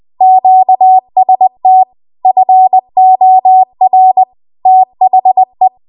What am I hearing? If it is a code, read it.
Morse code, heard as a keyed tone, about 20 wpm, 750 Hz. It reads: QST FOR THE
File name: Archive / 20 WPM / 20WPM